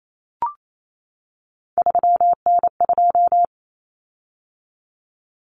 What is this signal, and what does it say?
Morse code, heard as a keyed tone, about 28 wpm, 700 Hz. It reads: 3N2
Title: Common Cut Numbers - 3-Digit 28wpm